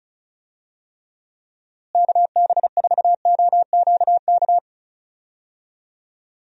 Morse code, heard as a keyed tone, about 34 wpm, 700 Hz. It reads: KB4OQK
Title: USA Call Signs - 6 Characters - Repeated 3x - 35wpm